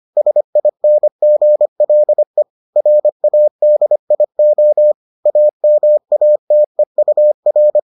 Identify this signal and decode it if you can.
Morse code, heard as a keyed tone, about 25 wpm, 600 Hz. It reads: SINGLE RADIO AMATEUR